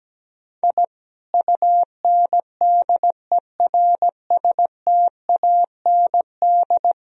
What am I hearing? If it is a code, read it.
Morse code, heard as a keyed tone, about 17 wpm, 700 Hz. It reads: I UNDERSTAND
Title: Sentences from Top 1000 Words - Review Part 3 - 17wpm